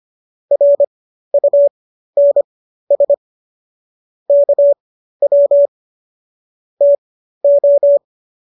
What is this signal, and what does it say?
Morse code, heard as a keyed tone, about 25 wpm, spaced as for 13 wpm, 570 Hz. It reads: RUNS KW TO
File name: qso214 13